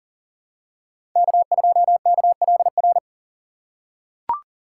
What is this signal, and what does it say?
Morse code, heard as a keyed tone, about 40 wpm, 700 Hz. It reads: K2KLR
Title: USA Call Signs - 2x Character Spacing - 40wpm